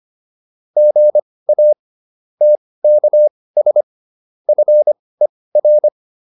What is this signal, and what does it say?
Morse code, heard as a keyed tone, about 25 wpm, spaced as for 18 wpm, 600 Hz. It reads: GA TKS FER